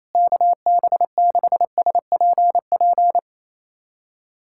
Morse code, heard as a keyed tone, about 28 wpm, 700 Hz. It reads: KB6SPP
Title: USA Callsign DE Callsign 28wpm